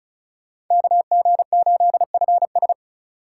Morse code, heard as a keyed tone, about 43 wpm, 700 Hz. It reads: KG8FS
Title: Speed Racing - USA 2x2 Call Signs 35wpm